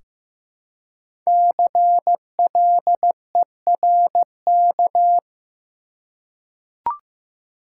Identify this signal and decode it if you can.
Morse code, heard as a keyed tone, about 15 wpm, 700 Hz. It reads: CLERK
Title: Five Letter Words - 15wpm